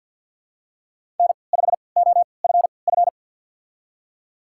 Morse code, heard as a keyed tone, about 50 wpm, spaced as for 28 wpm, 700 Hz. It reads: N5OFF
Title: USA Call Signs - 5 Characters - 3x Character Spacing - 50wpm